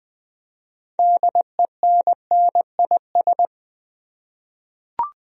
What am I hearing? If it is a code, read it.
Morse code, heard as a keyed tone, about 20 wpm, 700 Hz. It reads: DENNIS
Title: USA Common Names - 20wpm